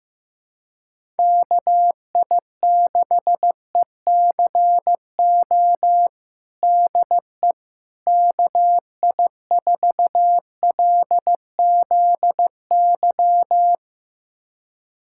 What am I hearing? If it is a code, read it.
Morse code, heard as a keyed tone, about 15 wpm, 700 Hz. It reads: KI6ECO DE KI4LZY
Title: USA Callsign DE Callsign - 6 Characters - 15wpm